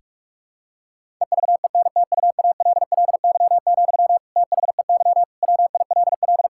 Morse code, heard as a keyed tone, about 46 wpm, 700 Hz. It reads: EVENTUALLY, THEY WILL
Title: Sentences from Top 1000 Words - Review Part 6 - 45wpm